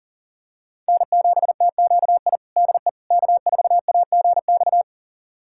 Morse code, heard as a keyed tone, about 37 wpm, 700 Hz. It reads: N7TQI DE K4AGX
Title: USA Callsign DE Callsign 40wpm